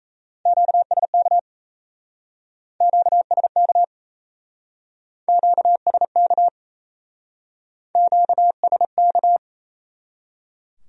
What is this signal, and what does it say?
Morse code, heard as a keyed tone, about 35 wpm, 700 Hz. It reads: QSK QSK QSK QSK
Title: Speed-Racing Q-Codes - 28wpm